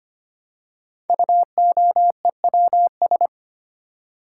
Morse code, heard as a keyed tone, about 25 wpm, 700 Hz. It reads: UOEWS
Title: Code Groups - 5 Letters - 25wpm